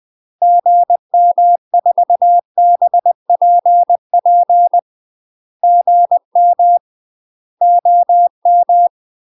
Morse code, heard as a keyed tone, about 20 wpm, 700 Hz. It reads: GM4BPP GM OM
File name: MASTER QSOs TRAINING/20WPM